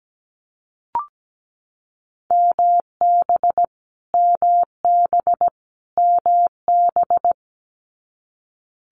Morse code, heard as a keyed tone, about 17 wpm, 700 Hz. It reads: MB MB MB
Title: Multi-letter Phonemes Repeated 3x 17wpm